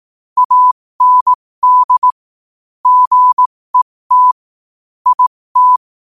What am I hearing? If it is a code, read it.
Morse code, heard as a keyed tone, about 18 wpm, spaced as for 15 wpm, 1000 Hz. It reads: AND GET IT
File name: qotdfiles / 15wpm